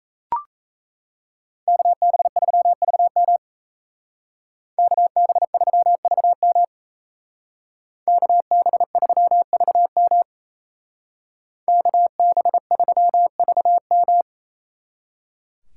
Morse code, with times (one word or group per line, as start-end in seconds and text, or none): 1.7-3.4 KB3VM
4.8-6.6 KB3VM
8.1-10.2 KB3VM
11.7-14.2 KB3VM